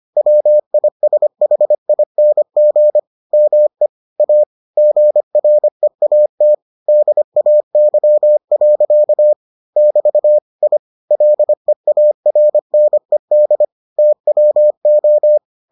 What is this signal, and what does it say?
Morse code, heard as a keyed tone, about 25 wpm, 600 Hz. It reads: WISHING ME A GREAT DAY. <BT> I LEARNED TWO